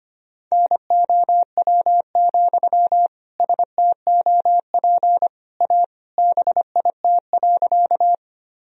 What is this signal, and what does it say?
Morse code, heard as a keyed tone, about 25 wpm, 700 Hz. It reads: NOW, STOP A BIT.
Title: Sentences from Top 1000 Words - Review Part 1 - 25wpm